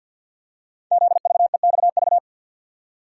Morse code, heard as a keyed tone, about 50 wpm, 700 Hz. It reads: ZVEXV
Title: Code Groups - 5 Letters - 50wpm